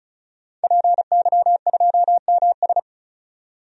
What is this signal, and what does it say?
Morse code, heard as a keyed tone, about 34 wpm, 700 Hz. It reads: PY2MS
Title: DXCC-100 Call Signs - 5 Characters - 35wpm